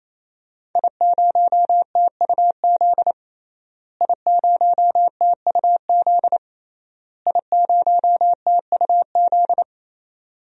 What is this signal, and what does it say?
Morse code, heard as a keyed tone, about 28 wpm, 700 Hz. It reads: I0TUZ I0TUZ I0TUZ
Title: DXCC-100 Call Signs - 5 Characters - Repeated 3x - 28wpm